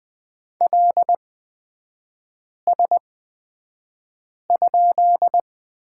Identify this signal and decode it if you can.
Morse code, heard as a keyed tone, about 20 wpm, spaced as for 5 wpm, 700 Hz. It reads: LS?
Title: 224 Lesson 36e - BT - Three Characters 20wpm